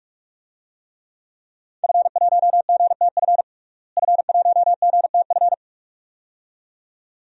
Morse code, heard as a keyed tone, about 47 wpm, 700 Hz. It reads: F1GTF F1GTF
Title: DXCC-100 Call Signs - 5 Characters - Repeated 2x - 45wpm